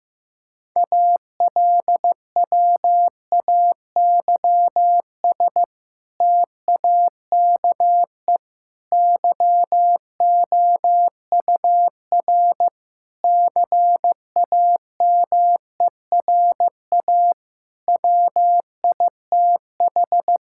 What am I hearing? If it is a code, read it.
Morse code, heard as a keyed tone, about 16 wpm, 700 Hz. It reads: ALWAYS TAKE YOUR CAMERA WITH
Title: Sentences from Top 1000 Words - Review Part 5 - 15wpm